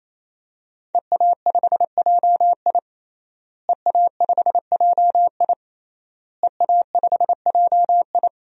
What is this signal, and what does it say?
Morse code, heard as a keyed tone, about 28 wpm, 700 Hz. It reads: EA5JI EA5JI EA5JI
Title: DXCC-100 Call Signs - 5 Characters - Repeated 3x - 28wpm